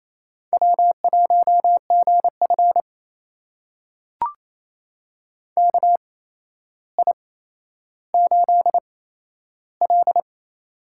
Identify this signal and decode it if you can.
Morse code, heard as a keyed tone, about 28 wpm, spaced as for 12 wpm, 700 Hz. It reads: W1GF KI8L
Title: USA Call Signs - 4 Characters - 8x Character Spacing - 28wpm